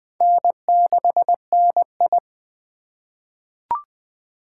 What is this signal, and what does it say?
Morse code, heard as a keyed tone, about 20 wpm, 700 Hz. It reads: N6NI
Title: USA Call Signs - 4 Characters - 4x Character Spacing - 20wpm